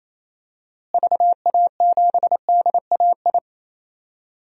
Morse code, heard as a keyed tone, about 28 wpm, 700 Hz. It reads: VA7DAI
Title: DXCC-100 Call Signs - 6 Characters - Repeated 3x - 28wpm